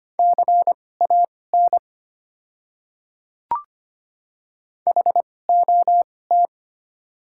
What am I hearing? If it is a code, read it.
Morse code, heard as a keyed tone, about 25 wpm, spaced as for 18 wpm, 690 Hz. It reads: CAN HOT
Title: 2x Character Spacing - Top 100 Words - 25wpm